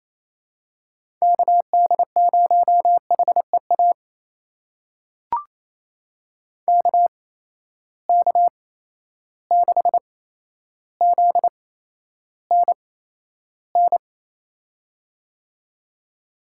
Morse code, heard as a keyed tone, about 28 wpm, spaced as for 12 wpm, 700 Hz. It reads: KD0HEA KK6ZNN
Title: USA Call Signs - 6 Characters - 8x Character Spacing - 28wpm